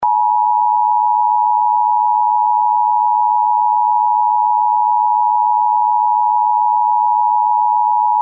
Emergency Alert System (SAME) tones (the 853 and 960 Hz attention tone). It is heard from 0.0 to 8.2 s.